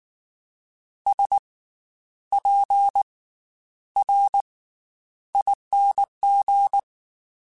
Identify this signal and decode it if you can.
Morse code, heard as a keyed tone, about 19 wpm, spaced as for 10 wpm, 800 Hz. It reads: SPRING
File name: Special INGs / Special INGs-mp3s / Slow ing @18